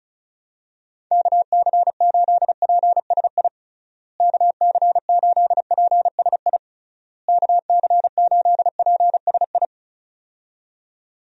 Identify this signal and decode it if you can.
Morse code, heard as a keyed tone, about 36 wpm, 700 Hz. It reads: KC8PSI KC8PSI KC8PSI
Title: USA Call Signs - 6 Characters - Repeated 3x - 35wpm